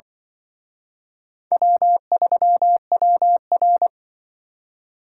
Morse code, heard as a keyed tone, about 24 wpm, 700 Hz. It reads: W3WR
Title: Speed Racing - USA Call Signs 20wpm